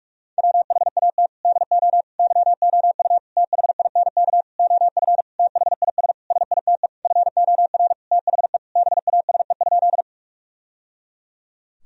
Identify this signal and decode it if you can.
Morse code, heard as a keyed tone, about 48 wpm, 700 Hz. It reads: WHAT DO YOU THINK OF THIS SITE FOR THE BASE?
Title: Sentences from Top 1000 Words - Review Part 1 - 45wpm